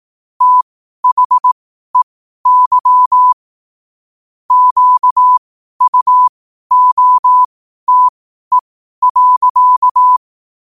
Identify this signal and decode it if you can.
Morse code, heard as a keyed tone, about 18 wpm, spaced as for 12 wpm, 1000 Hz. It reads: THEY QUOTE.
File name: qotdfiles / 12wpm